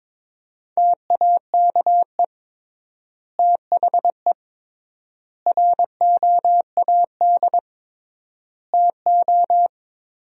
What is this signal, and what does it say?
Morse code, heard as a keyed tone, about 22 wpm, 700 Hz. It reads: TAKE THE ROAD TO
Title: 3x Word-Spacing - Sentences from Top 400 Words - 22wpm